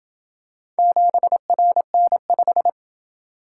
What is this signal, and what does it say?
Morse code, heard as a keyed tone, about 27 wpm, 700 Hz. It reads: 7RN5